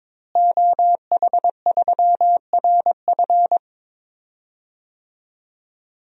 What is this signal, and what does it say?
Morse code, heard as a keyed tone, about 22 wpm, 700 Hz. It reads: OH3RF
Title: DXCC-100 Call Signs - 5 Characters - Repeated 2x - 22wpm